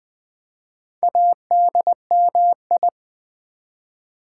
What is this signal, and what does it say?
Morse code, heard as a keyed tone, about 20 wpm, 700 Hz. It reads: ADMI
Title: Build A Word - 1 Letter At A Time - 5-Letter Words - 20wpm